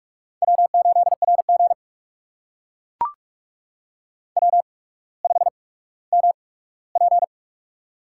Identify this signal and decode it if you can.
Morse code, heard as a keyed tone, about 44 wpm, spaced as for 19 wpm, 700 Hz. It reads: W8RG W5MP
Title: USA Call Signs - 4 Characters - 8x Character Spacing - 45wpm